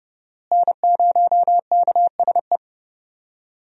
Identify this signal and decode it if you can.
Morse code, heard as a keyed tone, about 30 wpm, 700 Hz. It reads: N0KSE
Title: USA Call Signs - 5 Characters - Repeated 2x - 30wpm